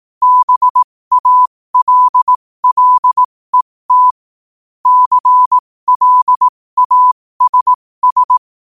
Morse code, heard as a keyed tone, about 18 wpm, spaced as for 15 wpm, 1000 Hz. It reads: BALLET CLASS
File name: qotdfiles / 15wpm